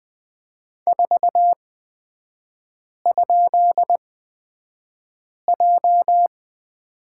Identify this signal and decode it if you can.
Morse code, heard as a keyed tone, about 20 wpm, spaced as for 5 wpm, 700 Hz. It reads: 4?J
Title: 224 Lesson 36e - BT - Three Characters 20wpm